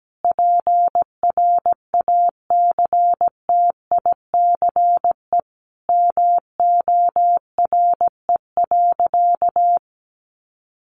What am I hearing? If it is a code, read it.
Morse code, heard as a keyed tone, about 17 wpm, 700 Hz. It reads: PRACTICE MORE.
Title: Sentences from Top 700 Words 17wpm